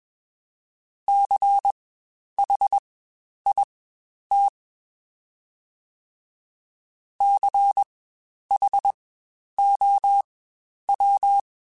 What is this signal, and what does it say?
Morse code, heard as a keyed tone, about 21 wpm, spaced as for 10 wpm, 800 Hz. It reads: CHIT CHOW
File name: Spaced Alphas / Spaced Alphas mp3s / Cs sp @20